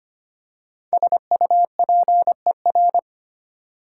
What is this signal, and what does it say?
Morse code, heard as a keyed tone, about 25 wpm, 700 Hz. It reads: SUPER
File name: 5 Letter Words - First 3 Letters Given - 25wpm